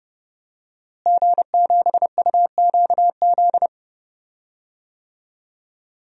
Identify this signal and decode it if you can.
Morse code, heard as a keyed tone, about 30 wpm, 700 Hz. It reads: G7UQZ
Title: DXCC-100 Call Signs - 5 Characters - Repeated 2x - 30wpm